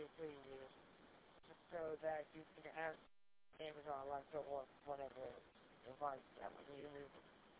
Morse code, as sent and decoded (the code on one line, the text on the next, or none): none